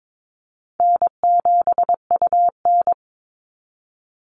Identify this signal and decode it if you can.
Morse code, heard as a keyed tone, about 22 wpm, 700 Hz. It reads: N7UN
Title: NA SOTA Activator Callsigns 22wpm